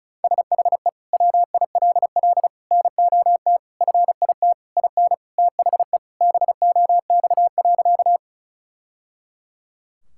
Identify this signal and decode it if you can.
Morse code, heard as a keyed tone, about 36 wpm, 700 Hz. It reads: SHE WILL NOT FIT IN THE BOX.
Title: Sentences from Top 1000 Words - Review Part 5 - 35wpm